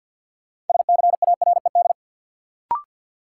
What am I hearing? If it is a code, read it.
Morse code, heard as a keyed tone, about 50 wpm, 700 Hz. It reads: SCARED
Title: Sentences from Top 700 Words - New Word Drill - 50wpm